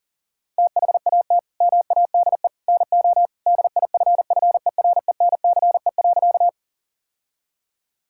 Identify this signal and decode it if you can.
Morse code, heard as a keyed tone, about 40 wpm, 700 Hz. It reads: THAT MADE NO DIFFERENCE.
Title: Sentences from Top 1000 Words - Review Part 5 - 40wpm